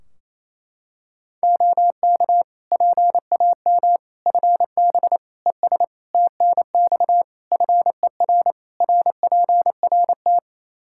Morse code, heard as a keyed tone, about 27 wpm, 700 Hz. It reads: OK PAM FB ES TNX FER RPRT
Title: K1USN SST QSO Practice 28wpm